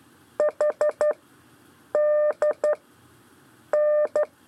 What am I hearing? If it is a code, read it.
Morse code, heard as a keyed tone, about 11 wpm, 600 Hz. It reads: H D N
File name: 4 LEZIONE CW 1